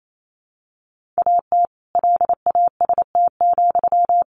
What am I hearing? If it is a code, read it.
Morse code, heard as a keyed tone, about 28 wpm, 700 Hz. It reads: AT LAST,
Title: Sentences from Top 1400 Words 28wpm